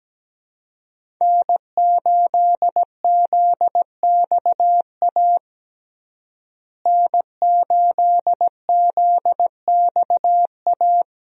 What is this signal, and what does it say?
Morse code, heard as a keyed tone, about 17 wpm, 700 Hz. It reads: N8ZXA N8ZXA
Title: USA Call Signs - 5 Characters - Repeated 3x - 17wpm